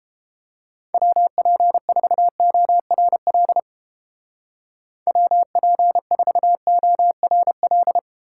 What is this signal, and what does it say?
Morse code, heard as a keyed tone, about 30 wpm, 700 Hz. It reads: WP4ORL WP4ORL